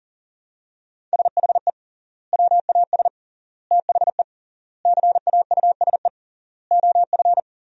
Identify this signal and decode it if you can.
Morse code, heard as a keyed tone, about 37 wpm, 700 Hz. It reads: SHE WAS THE CAUSE OF
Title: 3x Word-Spacing - Sentences from Top 200 Words - 40wpm